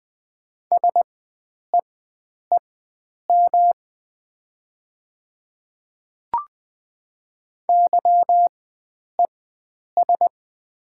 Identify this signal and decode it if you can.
Morse code, heard as a keyed tone, about 20 wpm, spaced as for 9 wpm, 700 Hz. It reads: SEEM YES
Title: 4x Character Spacing - Top 500 Words - 20wpm